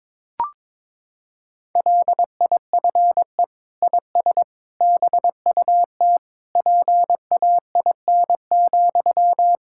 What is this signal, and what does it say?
Morse code, heard as a keyed tone, about 22 wpm, 700 Hz. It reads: LIFE IS BUT PAIN,
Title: Sentences from Top 1600 Words 22wpm